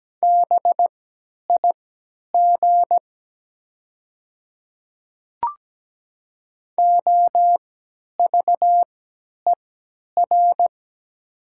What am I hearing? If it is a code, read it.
Morse code, heard as a keyed tone, about 17 wpm, spaced as for 10 wpm, 700 Hz. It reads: BIG OVER
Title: 3x Character Spacing - Top 200 Words - 17wpm